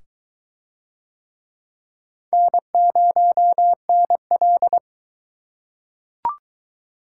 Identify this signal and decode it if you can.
Morse code, heard as a keyed tone, about 19 wpm, 700 Hz. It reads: N0NL